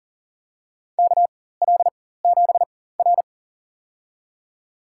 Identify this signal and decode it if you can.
Morse code, heard as a keyed tone, about 40 wpm, spaced as for 19 wpm, 690 Hz. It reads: KL7R
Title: USA Call Signs - 4x Character Spacing - 40wpm